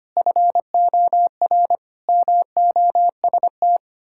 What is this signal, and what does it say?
Morse code, heard as a keyed tone, about 25 wpm, 700 Hz. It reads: FOR MOST
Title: Pangram Challenge v2 - 25wpm